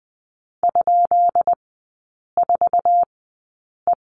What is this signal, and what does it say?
Morse code, heard as a keyed tone, about 20 wpm, spaced as for 8 wpm, 700 Hz. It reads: ?4E
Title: 133 Lesson 22e - Question Mark - Three Characters 20wpm